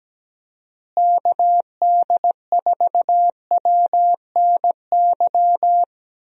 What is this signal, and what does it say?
Morse code, heard as a keyed tone, about 17 wpm, 700 Hz. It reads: KD4WNY
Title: USA Call Signs - 6 Characters - Repeated 2x - 17wpm